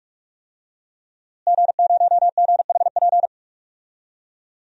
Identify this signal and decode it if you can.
Morse code, heard as a keyed tone, about 46 wpm, 700 Hz. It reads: G0GHP
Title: DXCC-100 Call Signs - 5 Characters - Repeated 2x - 45wpm